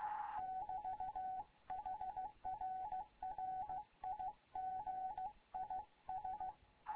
Morse code, heard as a keyed tone, about 15 wpm, 750 Hz, with radio noise. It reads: <BT> HR RIG IS
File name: Morze-HA2MN / hangp / 2 11 GYAKORLAT